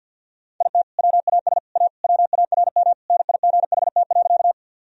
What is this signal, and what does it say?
Morse code, heard as a keyed tone, about 51 wpm, 700 Hz. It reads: IT WAS A WARM NIGHT.
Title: Sentences from Top 1000 Words - Review Part 2 - 50wpm